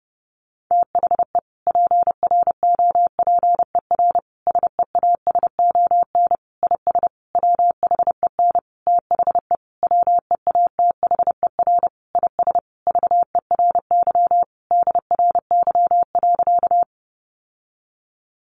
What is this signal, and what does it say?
Morse code, heard as a keyed tone, about 30 wpm, 700 Hz. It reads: THE PROPER SEASON IS WHEN THE WEATHER IS VERY DRY.
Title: Sentences from Top 1400 Words 30wpm